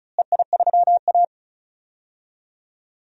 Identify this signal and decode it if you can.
Morse code, heard as a keyed tone, about 34 wpm, 700 Hz. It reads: EI3A
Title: DXCC-100 Call Signs - 4 Characters - Repeated 2x - 35wpm